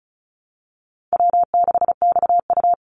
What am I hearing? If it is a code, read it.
Morse code, heard as a keyed tone, about 34 wpm, 700 Hz. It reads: W6XU
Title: Easy Call Signs Repeated 2x - 35wpm